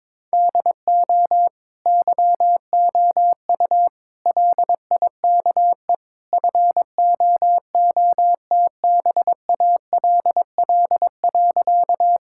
Morse code, heard as a keyed tone, about 22 wpm, 700 Hz. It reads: DO YOU LIKE FOOTBALL.
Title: Sentences from Top 1600 Words 22wpm